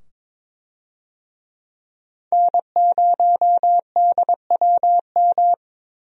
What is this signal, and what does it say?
Morse code, heard as a keyed tone, about 22 wpm, 700 Hz. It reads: N0DWM
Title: USA Call Signs - 5 Characters - 22wpm